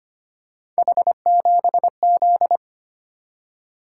Morse code, heard as a keyed tone, about 25 wpm, 700 Hz. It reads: H7Z
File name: Three Letter-Number Combinations - 25wpm